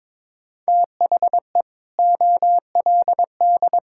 Morse code, heard as a keyed tone, about 22 wpm, 700 Hz. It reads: THE OLD
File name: practice / 22wpm / Sentences with 2-Word Phrases - 22wpm